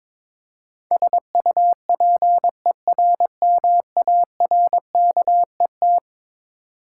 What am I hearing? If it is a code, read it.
Morse code, heard as a keyed tone, about 22 wpm, 700 Hz. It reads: SUPERMARKET
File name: practice / 22wpm / Eleven Letter Words - 22wpm